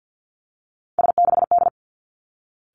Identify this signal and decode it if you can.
Morse code, heard as a keyed tone, about 50 wpm, 700 Hz. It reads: S6D
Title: Three Challenging Letter-Number Combinations 50wpm